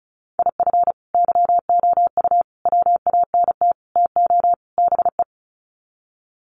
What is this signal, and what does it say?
Morse code, heard as a keyed tone, about 34 wpm, 700 Hz. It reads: IF YOU WANT TO BE
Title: Sets of 5 Words Encore 35wpm